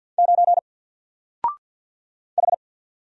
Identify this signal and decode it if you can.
Morse code, heard as a keyed tone, about 48 wpm, 700 Hz. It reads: 9 H
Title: Single Character (Letter:Number:Punctuation:Prosign) 50wpm